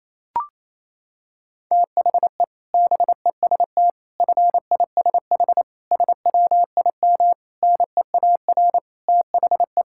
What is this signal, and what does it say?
Morse code, heard as a keyed tone, about 28 wpm, 700 Hz. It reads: THE BEST FISH SWIM NEAR THE
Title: Sentences from Top 1000 Words - Review Part 6 - 28wpm